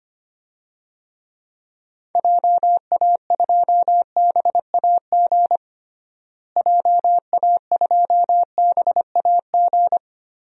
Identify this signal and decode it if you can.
Morse code, heard as a keyed tone, about 25 wpm, 700 Hz. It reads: JA2BAG JA2BAG
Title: DXCC-100 Call Signs - 6 Characters - Repeated 3x - 25wpm